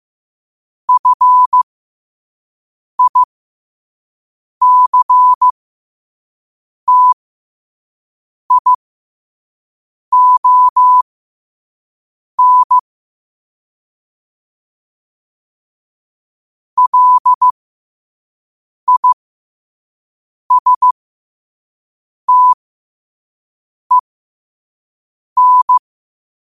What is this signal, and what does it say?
Morse code, heard as a keyed tone, about 15 wpm, spaced as for 5 wpm, 1000 Hz. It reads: FICTION LISTEN
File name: qotdfiles / 5wpm